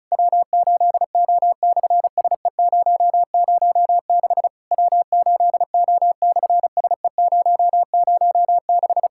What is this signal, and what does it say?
Morse code, heard as a keyed tone, about 36 wpm, 700 Hz. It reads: W8O/SE006 W8O/SE006
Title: SOTA QSO Practice 35wpm